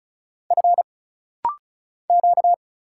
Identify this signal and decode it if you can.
Morse code, heard as a keyed tone, about 36 wpm, 700 Hz. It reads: F Q
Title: Single Letters - Rapid-Fire 35wpm